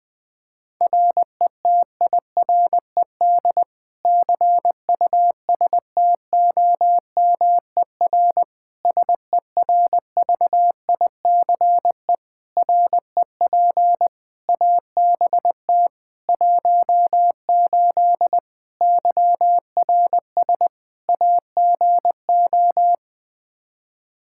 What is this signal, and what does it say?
Morse code, heard as a keyed tone, about 20 wpm, 700 Hz. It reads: RETIRED CUSTOMER SERVICE REP ABT 18 YRS AGO
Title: Practice QSO - Exchange 3 - More Info - 20wpm